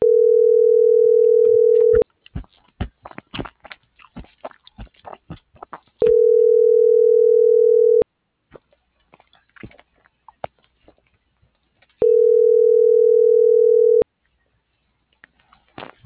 Ringback tone (North America), 440 + 480 Hz.